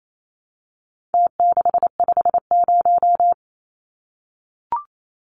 Morse code, heard as a keyed tone, about 28 wpm, 700 Hz. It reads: T650
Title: Numbers - Contest Style - 4-Digit 28wpm